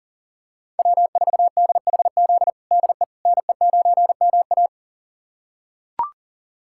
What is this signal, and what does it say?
Morse code, heard as a keyed tone, about 40 wpm, 700 Hz. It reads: W4DHZ DE NE9MA
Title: USA Callsign DE Callsign - 5 Characters - 40wpm